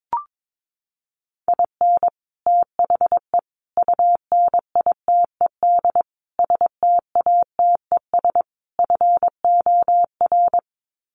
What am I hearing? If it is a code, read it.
Morse code, heard as a keyed tone, about 22 wpm, 700 Hz. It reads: IN THE UNITED STATES FOR
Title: Sets of 5 Words Encore2 22wpm